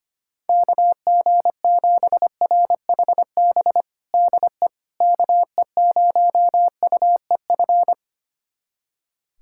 Morse code, heard as a keyed tone, about 25 wpm, 700 Hz. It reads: KG7RHB DE KE0UEF
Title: USA Callsign DE Callsign - 6 Characters - 25wpm